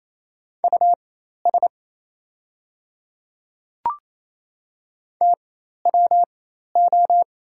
Morse code, heard as a keyed tone, about 27 wpm, spaced as for 13 wpm, 690 Hz. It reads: US TWO
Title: 4x Character Spacing - Top 300 Words - 28wpm